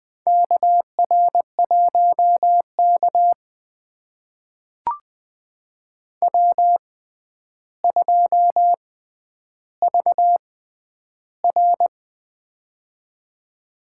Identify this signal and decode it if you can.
Morse code, heard as a keyed tone, about 20 wpm, spaced as for 10 wpm, 700 Hz. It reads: KR1K W2VR
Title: USA Call Signs - 4 Characters - 6x Character Spacing - 20wpm